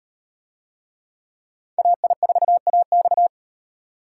Morse code, heard as a keyed tone, about 34 wpm, 700 Hz. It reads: AI4AX